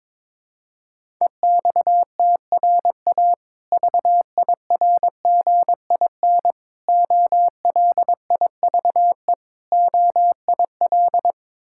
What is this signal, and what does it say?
Morse code, heard as a keyed tone, about 22 wpm, 700 Hz. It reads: EXTRA VIRGIN OLIVE OIL
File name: Difficult Words with Hints - 22wpm